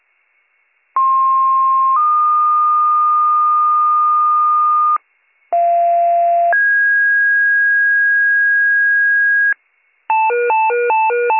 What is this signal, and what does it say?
Two-tone page, heard as a one-second tone, then a three-second tone, about 1060 then 1220 Hz, about 690 then 1650 Hz, with noise. It is heard from 1.0 to 9.5 s.